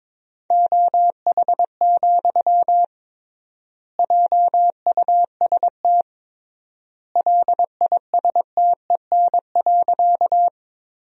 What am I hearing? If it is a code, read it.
Morse code, heard as a keyed tone, about 22 wpm, 700 Hz. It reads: OH, JUST LISTEN.
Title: 3x Word-Spacing - Sentences from Top 500 Words - 22wpm